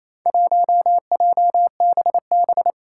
Morse code, heard as a keyed tone, about 27 wpm, 700 Hz. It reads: 1JBB
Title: Four Challenging Letter-Number Combinations 28wpm